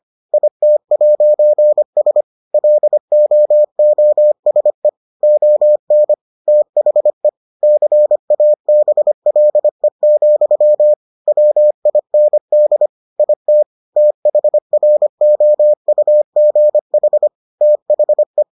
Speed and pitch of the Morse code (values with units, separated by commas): 25 wpm, 600 Hz